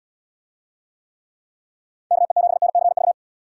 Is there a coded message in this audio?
Yes: Morse code from 2.1 to 3.1 s, heard as a keyed tone, about 74 wpm, 700 Hz. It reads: KE7NCV